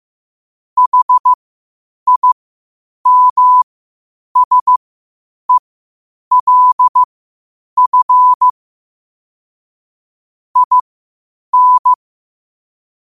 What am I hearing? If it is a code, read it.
Morse code, heard as a keyed tone, about 15 wpm, spaced as for 8 wpm, 1000 Hz. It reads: HIMSELF IN
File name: qotdfiles / 8wpm